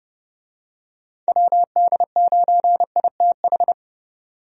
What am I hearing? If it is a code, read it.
Morse code, heard as a keyed tone, about 32 wpm, 700 Hz. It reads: WD9ITH
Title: USA Call Signs - 6 Characters - 30wpm